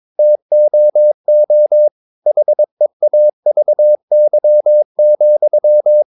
Morse code, heard as a keyed tone, about 22 wpm, 600 Hz. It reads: TOO HEAVY,
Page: Romanen Never af Ken Follet på engelsk.